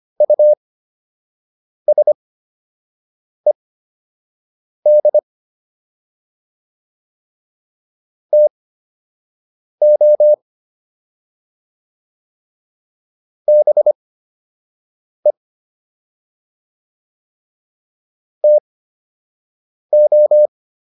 Morse code, heard as a keyed tone, about 25 wpm, spaced as for 6 wpm, 600 Hz. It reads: USED TO BE TO